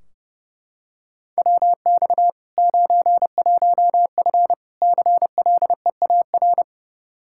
Morse code, heard as a keyed tone, about 30 wpm, 700 Hz. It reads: WX 91F CLEAR
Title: Practice QSO - Exchange 2 - Info - 30wpm